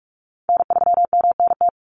Morse code, heard as a keyed tone, about 44 wpm, 700 Hz. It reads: N3MNT
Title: Call Signs Repeated 2x - 45wpm